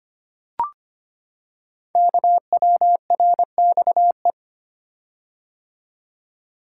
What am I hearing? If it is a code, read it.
Morse code, heard as a keyed tone, about 25 wpm, 700 Hz. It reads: KWRXE
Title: Code Groups - 5 Letters - 25wpm